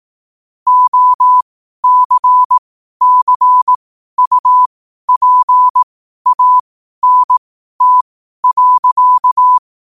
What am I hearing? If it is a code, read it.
Morse code, heard as a keyed tone, about 18 wpm, spaced as for 12 wpm, 1000 Hz. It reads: OCCUPANT.
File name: qotdfiles / 12wpm